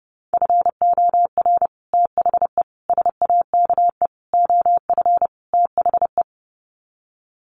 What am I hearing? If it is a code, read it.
Morse code, heard as a keyed tone, about 30 wpm, 700 Hz. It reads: FOR THE SAKE OF THE
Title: Sets of 5 Words Encore 30wpm